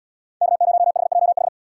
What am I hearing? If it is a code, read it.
Morse code, heard as a keyed tone, about 71 wpm, 700 Hz. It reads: K0HJH